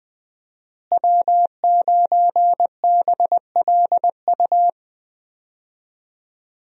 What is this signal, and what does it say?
Morse code, heard as a keyed tone, about 20 wpm, 700 Hz. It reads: W9BLU
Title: USA Call Signs - 5 Characters - Repeated 2x - 20wpm